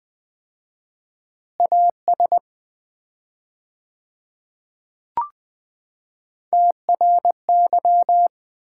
Morse code, heard as a keyed tone, about 21 wpm, 700 Hz. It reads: AS TRY
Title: 4x Word-Spacing - Sets of 2 Words - 20wpm